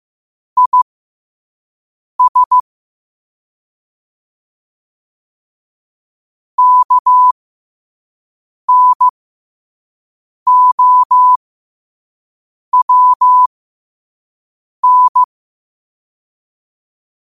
Morse code, heard as a keyed tone, about 15 wpm, spaced as for 5 wpm, 1000 Hz. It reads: IS KNOWN
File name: qotdfiles / 5wpm